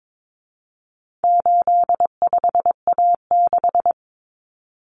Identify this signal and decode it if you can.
Morse code, heard as a keyed tone, about 22 wpm, 700 Hz. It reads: 85A6
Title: Common Cut Numbers - 4-Digit 22wpm